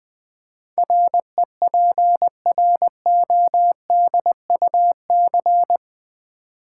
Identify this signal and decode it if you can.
Morse code, heard as a keyed tone, about 20 wpm, 700 Hz. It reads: REPRODUC
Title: Build A Word - 4 Letters At A Time - 12-Letter Words - 20wpm